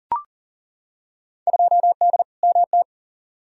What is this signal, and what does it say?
Morse code, heard as a keyed tone, about 40 wpm, 700 Hz. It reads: 2D MT
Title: Field Day Exchanges Encore - 40wpm